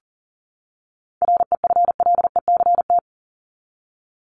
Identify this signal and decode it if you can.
Morse code, heard as a keyed tone, about 40 wpm, 700 Hz. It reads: REFLECT
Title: Top 1251-1500 Words 40wpm